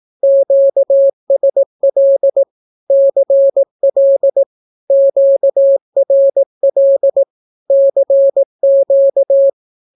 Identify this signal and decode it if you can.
Morse code, heard as a keyed tone, about 18 wpm, 550 Hz. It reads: QSL CL QRL CQ
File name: naukapodstaw / skroty / skroty